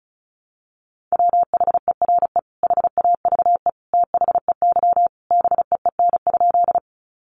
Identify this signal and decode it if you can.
Morse code, heard as a keyed tone, about 34 wpm, 700 Hz. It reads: WHERE HAVE THEY BEEN?
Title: Sentences from Top 200 Words 35wpm